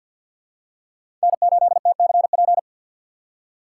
Morse code, heard as a keyed tone, about 50 wpm, 700 Hz. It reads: N8TCP
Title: USA Call Signs - 5 Characters - 6x Character Spacing - 50wpm